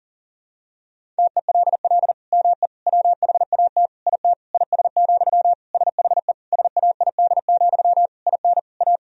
Morse code, heard as a keyed tone, about 40 wpm, 700 Hz. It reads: TELL ME WHAT IT IS, SHE SAID, IN A
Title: Sentences from Top 1000 Words - Review Part 1 - 40wpm